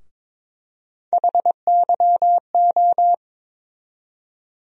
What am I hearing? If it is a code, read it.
Morse code, heard as a keyed tone, about 22 wpm, 700 Hz. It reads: HYO
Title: USA Call Sign Postfixes - 3 Letters 22wpm